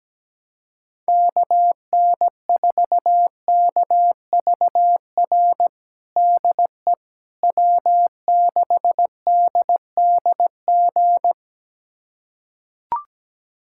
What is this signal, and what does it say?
Morse code, heard as a keyed tone, about 17 wpm, 700 Hz. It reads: KN4KVR DE W6DDG
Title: USA Callsign DE Callsign 17wpm